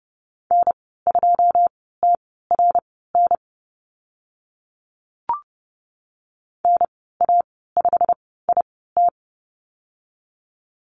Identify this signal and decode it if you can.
Morse code, heard as a keyed tone, about 31 wpm, spaced as for 17 wpm, 700 Hz. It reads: N2TRN NA5IT
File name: Lesson 15g - Number 5 - Call Signs 30wpm